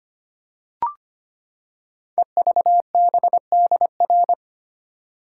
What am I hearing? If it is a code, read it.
Morse code, heard as a keyed tone, about 25 wpm, 700 Hz. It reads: EVBDR
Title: Code Groups - 5 Letter:Numbers - 25wpm